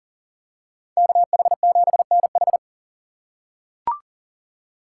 Morse code, heard as a keyed tone, about 40 wpm, 700 Hz. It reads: KH7NH
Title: USA Call Signs - 5 Characters - 4x Character Spacing - 40wpm